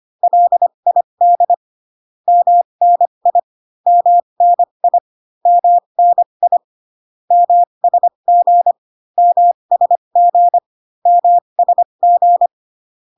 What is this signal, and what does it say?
Morse code, heard as a keyed tone, about 25 wpm, spaced as for 22 wpm, 700 Hz. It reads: LID MNI MNI MNI MSG MSG MSG
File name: CW common abbreviations alphabetical repeated 3 times 25 18wpm